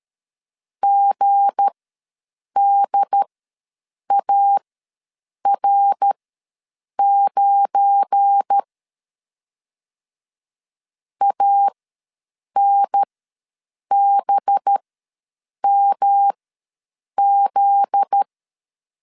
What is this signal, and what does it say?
Morse code, heard as a keyed tone, about 13 wpm, spaced as for 7 wpm, 800 Hz. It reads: GDAR9 ANBMZ
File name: LEZIONI / 7 GF9Z / 7 GF9Z ws 12wpm 3321